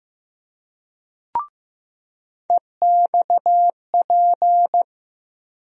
Morse code, heard as a keyed tone, about 15 wpm, 700 Hz. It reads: EXP
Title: USA Call Sign Postfixes - 3 Letters 15wpm